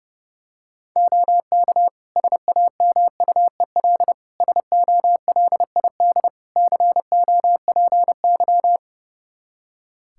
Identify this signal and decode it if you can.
Morse code, heard as a keyed tone, about 30 wpm, 700 Hz. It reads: OK SAMUEL SOLID COPY
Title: K1USN SST QSO Practice 30wpm